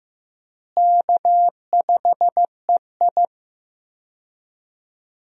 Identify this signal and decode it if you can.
Morse code, heard as a keyed tone, about 15 wpm, 700 Hz. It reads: K5EI
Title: USA Call Signs - 4 Characters - Repeated 3x - 15wpm